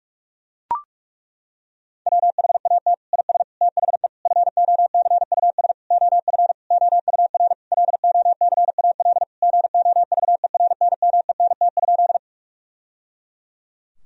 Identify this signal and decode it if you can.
Morse code, heard as a keyed tone, about 41 wpm, 700 Hz. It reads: WHAT IS THE FOCUS OF OUR LOCAL GOVERNMENT?
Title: Sentences from Top 1000 Words - Review Part 4 - 45wpm